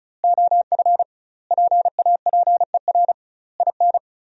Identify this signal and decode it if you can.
Morse code, heard as a keyed tone, about 34 wpm, 700 Hz. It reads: OF PAPER IN
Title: 2x Word-Spacing - Sentences from Top 400 Words - 35wpm